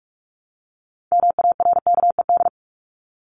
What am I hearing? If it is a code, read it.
Morse code, heard as a keyed tone, about 44 wpm, 700 Hz. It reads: MARKED
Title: ED Suffix - Top 100 Words with Base Words 45wpm